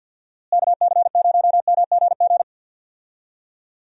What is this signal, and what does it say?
Morse code, heard as a keyed tone, about 52 wpm, 700 Hz. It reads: KC0MGG
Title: USA Call Signs - 6 Characters - Repeated 2x - 50wpm